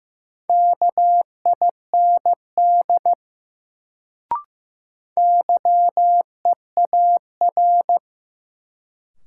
Morse code, heard as a keyed tone, about 16 wpm, 700 Hz. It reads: KIND YEAR
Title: Sentences from Top 200 Words - New Word Drill - 15wpm